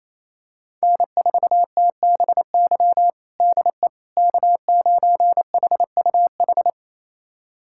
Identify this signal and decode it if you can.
Morse code, heard as a keyed tone, about 28 wpm, 700 Hz. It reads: N4TBY DE K9HUH
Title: USA Callsign DE Callsign - 5 Characters - 28wpm